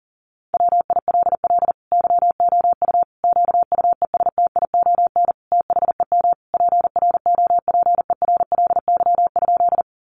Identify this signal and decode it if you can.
Morse code, heard as a keyed tone, about 40 wpm, 700 Hz. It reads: WILL YOU QUESTION THEM PROPERLY?
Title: Sentences from Top 1100 Words 40wpm